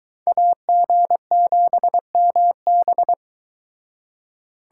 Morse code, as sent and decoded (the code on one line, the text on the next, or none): .- --. --... -- -...
AG7MB